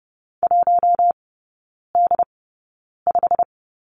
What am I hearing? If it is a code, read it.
Morse code, heard as a keyed tone, about 30 wpm, spaced as for 9 wpm, 700 Hz. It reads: 1D5
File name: Lesson 17e - Letter u - Three Characters 30wpm